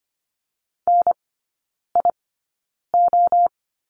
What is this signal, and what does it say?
Morse code, heard as a keyed tone, about 25 wpm, spaced as for 9 wpm, 690 Hz. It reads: NIO
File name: Lesson 15e - Number 5 - Three Characters 25wpm